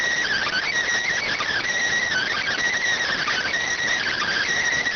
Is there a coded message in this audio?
Yes: Emergency Alert System (SAME) tones (header bursts) from 0.2 to 4.5 s, heard as harsh buzzing bursts.